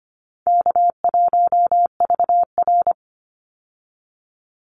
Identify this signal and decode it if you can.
Morse code, heard as a keyed tone, about 25 wpm, 700 Hz. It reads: K1VR
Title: Easy Call Signs Repeated 3x - 25wpm